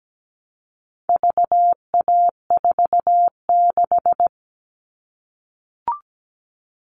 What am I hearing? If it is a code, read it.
Morse code, heard as a keyed tone, about 17 wpm, 700 Hz. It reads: VA46
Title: All Cut Numbers - 4-Digit 17wpm